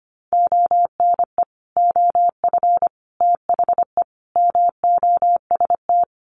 Morse code, heard as a keyed tone, about 25 wpm, 700 Hz. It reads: ONE OF THE MOST
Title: Sets of 4 Words 25wpm